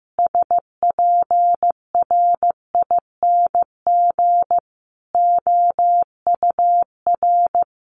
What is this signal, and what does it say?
Morse code, heard as a keyed tone, about 15 wpm, 700 Hz. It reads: SPRING OUR
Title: Sentences from Top 1200 Words 15wpm